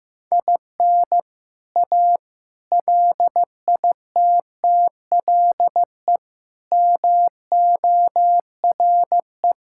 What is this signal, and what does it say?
Morse code, heard as a keyed tone, about 15 wpm, 700 Hz. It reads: IN A LITTLE MORE
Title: Sentences from Top 1000 Words - Review Part 5 - 15wpm